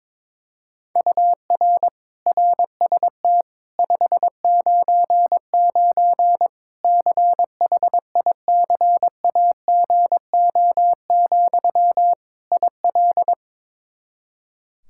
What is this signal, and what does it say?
Morse code, heard as a keyed tone, about 22 wpm, 700 Hz. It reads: UR RST 599 CHICAGO, IL
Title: Practice QSO - Exchange 1 - Beginning - 22wpm